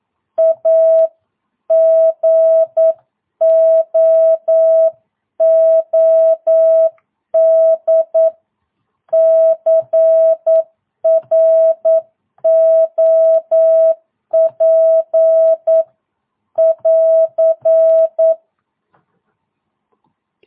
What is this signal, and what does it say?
Morse code, heard as a keyed tone, about 9 wpm, 650 Hz. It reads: A GOOD CROP <AR>
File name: jh8cbh / 2022cw